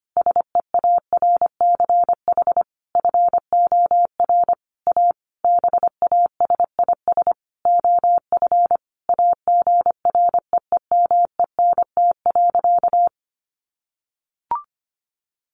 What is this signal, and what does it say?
Morse code, heard as a keyed tone, about 25 wpm, 700 Hz. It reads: SEARCH FOR A BASIS OF AGREEMENT.
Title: Sentences from Top 1500 Words 25wpm